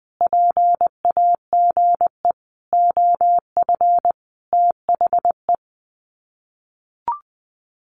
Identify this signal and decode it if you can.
Morse code, heard as a keyed tone, about 20 wpm, 700 Hz. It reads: PAGE OF THE
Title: Sets of 5 Words Encore 20wpm